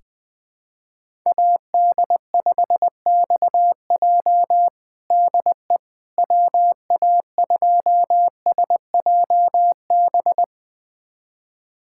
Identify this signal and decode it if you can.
Morse code, heard as a keyed tone, about 20 wpm, 700 Hz. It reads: AD5XJ DE WA2SJB
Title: USA Callsign DE Callsign 20wpm